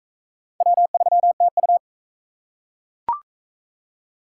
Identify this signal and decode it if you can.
Morse code, heard as a keyed tone, about 41 wpm, 700 Hz. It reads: W3TU